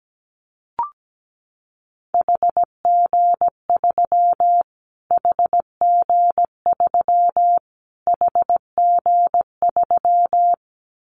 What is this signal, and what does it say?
Morse code, heard as a keyed tone, about 17 wpm, 700 Hz. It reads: HG3 HG3 HG3
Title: Callsign Prefixes for Top 25 DXCC - Repeated 3x - 17wpm